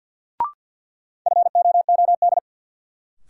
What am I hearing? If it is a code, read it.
Morse code, heard as a keyed tone, about 53 wpm, 700 Hz. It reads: FOOD
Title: Sentences from Top 300 Words - New Word Drill - 50wpm